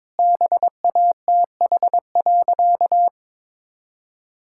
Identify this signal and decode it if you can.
Morse code, heard as a keyed tone, about 22 wpm, 700 Hz. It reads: BATH.
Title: Sentences from Top 1600 Words 22wpm